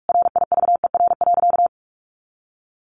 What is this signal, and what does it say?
Morse code, heard as a keyed tone, about 44 wpm, 700 Hz. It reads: RIVER.
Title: Sentences from Top 1400 Words 45wpm